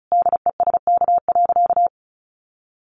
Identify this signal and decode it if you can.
Morse code, heard as a keyed tone, about 34 wpm, 700 Hz. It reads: DESK.
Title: A Sound Of Thunder 35wpm